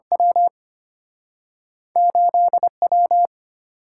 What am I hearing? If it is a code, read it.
Morse code, heard as a keyed tone, about 35 wpm, spaced as for 30 wpm, 700 Hz. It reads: W 8W
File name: practice / 25wpm / Speed-Racing - Two Letter-Number Combinations - 25wpm